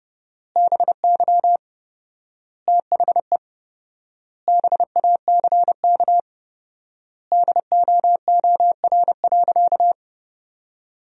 Morse code, heard as a keyed tone, about 30 wpm, 700 Hz. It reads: BY THE BACK DOOR.
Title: 4x Word-Spacing - Sentences from Top 300 Words - 30wpm